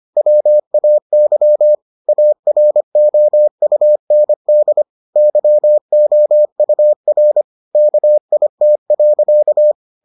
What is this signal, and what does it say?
Morse code, heard as a keyed tone, about 25 wpm, 600 Hz. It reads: WAY AROUND YOUR KIT.